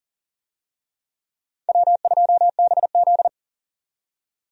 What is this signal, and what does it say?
Morse code, heard as a keyed tone, about 37 wpm, 700 Hz. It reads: W2BZ
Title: USA Call Signs - 4 Characters - 40wpm